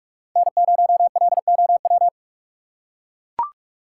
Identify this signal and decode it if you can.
Morse code, heard as a keyed tone, about 42 wpm, 700 Hz. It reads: N0LOW
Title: USA Call Signs - 5 Characters - 4x Character Spacing - 45wpm